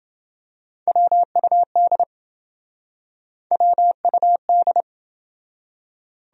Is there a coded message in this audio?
Yes: Morse code from 0.9 to 4.8 s, heard as a keyed tone, about 28 wpm, 700 Hz. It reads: WUD WUD